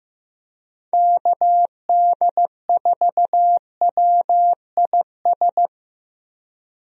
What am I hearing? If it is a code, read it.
Morse code, heard as a keyed tone, about 15 wpm, 700 Hz. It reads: KD4WIS
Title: USA Call Signs - 6 Characters - Repeated 2x - 15wpm